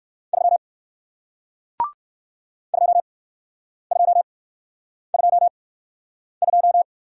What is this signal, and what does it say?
Morse code, heard as a keyed tone, about 60 wpm, spaced as for 9 wpm, 700 Hz. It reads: 3 2222